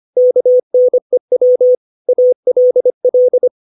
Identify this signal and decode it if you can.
Morse code, heard as a keyed tone, about 25 wpm, 500 Hz. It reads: KNEW ALL
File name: 632 words / 632 Words 25WPM